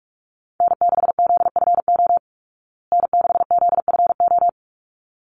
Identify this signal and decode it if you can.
Morse code, heard as a keyed tone, about 44 wpm, 700 Hz. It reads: N6ZFO N6ZFO
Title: Call Signs Repeated 3x - 45wpm